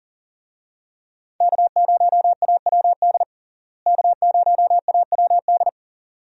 Morse code, heard as a keyed tone, about 37 wpm, 700 Hz. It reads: K0AWD K0AWD
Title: USA Call Signs - 5 Characters - Repeated 2x - 40wpm